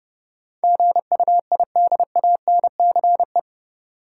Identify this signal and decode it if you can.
Morse code, heard as a keyed tone, about 30 wpm, 700 Hz. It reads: GUIDANCE
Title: Sentences from Top 1600 Words - New Word Drill Alt - 30wpm